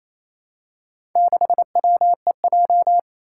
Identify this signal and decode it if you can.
Morse code, heard as a keyed tone, about 27 wpm, 700 Hz. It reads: 6WEJ
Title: Four Challenging Letter-Number Combinations 28wpm